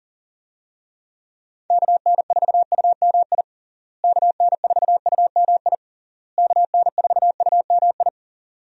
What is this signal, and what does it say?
Morse code, heard as a keyed tone, about 40 wpm, 700 Hz. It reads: KN4UMI KN4UMI KN4UMI
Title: USA Call Signs - 6 Characters - Repeated 3x - 40wpm